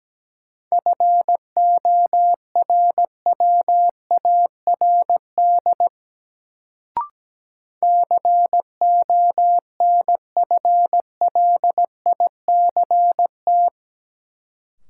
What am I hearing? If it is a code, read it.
Morse code, heard as a keyed tone, about 17 wpm, 700 Hz. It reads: FORWARD CONFLICT
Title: Sentences from Top 1300 Words - New Word Drill - 17wpm